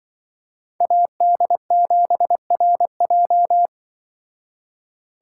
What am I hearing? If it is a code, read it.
Morse code, heard as a keyed tone, about 24 wpm, 700 Hz. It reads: AD7RJ
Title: Speed Racing - USA 2x2 Call Signs 20wpm